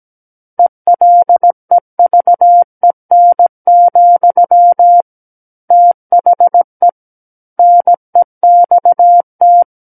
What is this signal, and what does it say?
Morse code, heard as a keyed tone, about 17 wpm, 700 Hz. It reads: ELEVEN, THE NEXT
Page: SKCC Learning Center - Morse Code Practice Files